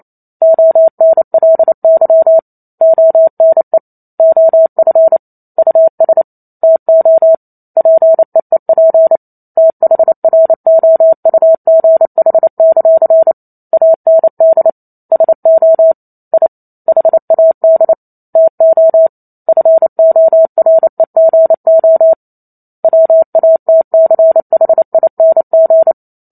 Morse code, heard as a keyed tone, about 28 wpm, 650 Hz. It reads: ONLY ONE OF US TO PEEP THROUGH; AND SO I HAD TO FOREGO WATCHING